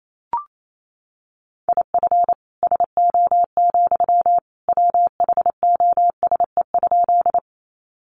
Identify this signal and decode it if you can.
Morse code, heard as a keyed tone, about 28 wpm, 700 Hz. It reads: IF SO, WHOSE?
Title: Sentences from Top 1200 Words 28wpm